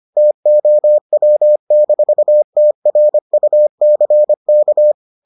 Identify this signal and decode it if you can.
Morse code, heard as a keyed tone, about 25 wpm, 600 Hz. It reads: TOW-TRUCK